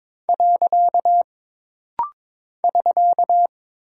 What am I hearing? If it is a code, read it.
Morse code, heard as a keyed tone, about 22 wpm, 700 Hz. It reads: . <SK>
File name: practice / 22wpm / Single Punctuation-Prosign - Rapid-Fire - 22wpm